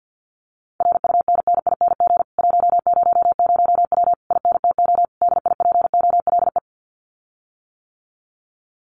Morse code, heard as a keyed tone, about 48 wpm, 700 Hz. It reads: RUNNING 100W INTO DIPOLE
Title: Common QSO Phrases Encore 50wpm